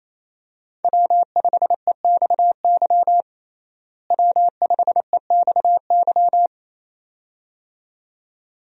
Morse code, heard as a keyed tone, about 27 wpm, 700 Hz. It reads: W5EXY W5EXY
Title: USA Call Signs - 5 Characters - Repeated 2x - 28wpm